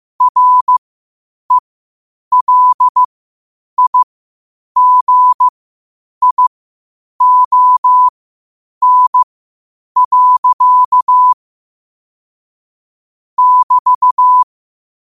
Morse code, heard as a keyed tone, about 15 wpm, spaced as for 8 wpm, 1000 Hz. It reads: RELIGION. <BT>
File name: qotdfiles / 8wpm